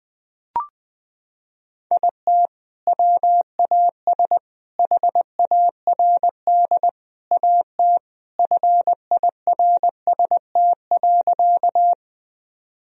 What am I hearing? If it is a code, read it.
Morse code, heard as a keyed tone, about 21 wpm, 700 Hz. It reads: IT WAS HARD AT FIRST.
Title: Sentences from Top 1000 Words - Review Part 1 - 20wpm